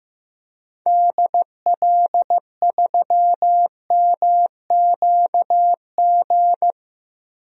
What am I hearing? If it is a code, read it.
Morse code, heard as a keyed tone, about 16 wpm, 700 Hz. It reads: DL3MQG
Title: DXCC-100 Call Signs - 6 Characters - Repeated 3x - 15wpm